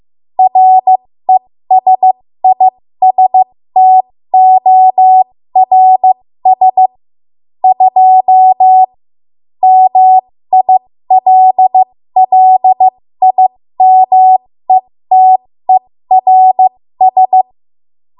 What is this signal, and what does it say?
Morse code, heard as a keyed tone, about 15 wpm, spaced as for 13 wpm, 750 Hz. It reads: RESISTORS 2 MILLIMETERS
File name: Morse / 13WPM